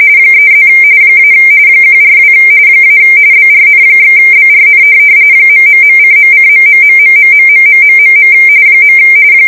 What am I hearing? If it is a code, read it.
RTTY, about 45.45 baud. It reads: 3+ - 142 - BRN/BRN ...:.:...........